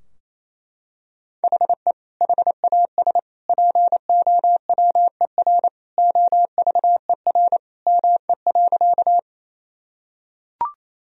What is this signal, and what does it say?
Morse code, heard as a keyed tone, about 28 wpm, 700 Hz. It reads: HE HAS POWER OVER ME.
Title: Sentences from Top 1000 Words - Review Part 2 - 28wpm